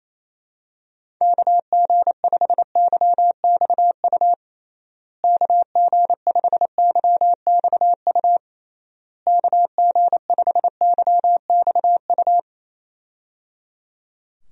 Morse code, heard as a keyed tone, about 27 wpm, 700 Hz. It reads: KG5YXU KG5YXU KG5YXU
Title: USA Call Signs - 6 Characters - Repeated 3x - 28wpm